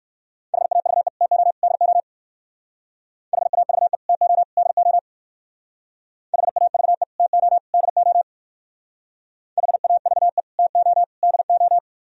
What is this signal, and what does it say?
Morse code, heard as a keyed tone, about 54 wpm, 700 Hz. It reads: HAVE TO DO HAVE TO DO HAVE TO DO HAVE TO DO
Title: Speed-Racing - Sets of 3 Words 45wpm